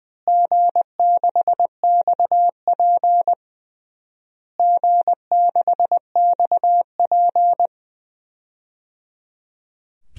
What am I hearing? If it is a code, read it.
Morse code, heard as a keyed tone, about 20 wpm, 700 Hz. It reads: G6XP G6XP
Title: DXCC-100 Call Signs - 4 Characters - Repeated 3x - 20wpm